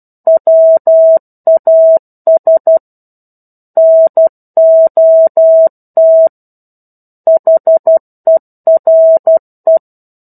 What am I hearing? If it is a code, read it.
Morse code, heard as a keyed tone, about 12 wpm, 650 Hz. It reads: WAS NOT HERE
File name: 01-Chapter-01 12 WPM